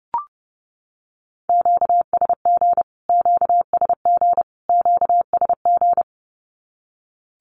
Morse code, heard as a keyed tone, about 30 wpm, 700 Hz. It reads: QSG QSG QSG
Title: Q-Codes Repeated 3x 30wpm